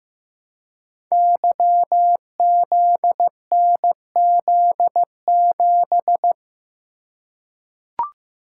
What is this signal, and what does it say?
Morse code, heard as a keyed tone, about 15 wpm, 700 Hz. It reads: YZNZ7
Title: Code Groups - 5 Letter:Numbers - 15wpm